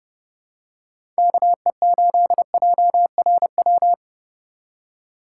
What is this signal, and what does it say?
Morse code, heard as a keyed tone, about 30 wpm, 700 Hz. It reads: KE8JRW
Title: USA Call Signs - 6 Characters - 3x Character Spacing - 30wpm